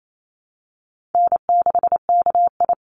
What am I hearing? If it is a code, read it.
Morse code, heard as a keyed tone, about 28 wpm, 700 Hz. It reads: N6KI
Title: Easy Call Signs - 28wpm